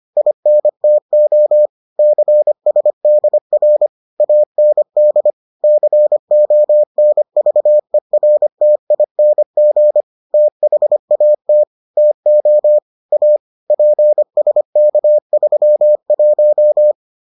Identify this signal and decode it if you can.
Morse code, heard as a keyed tone, about 25 wpm, 600 Hz. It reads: INTO CSDR AND CONVERTING THAT TO A PSK31